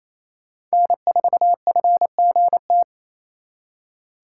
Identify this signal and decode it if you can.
Morse code, heard as a keyed tone, about 35 wpm, 700 Hz. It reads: N4FGT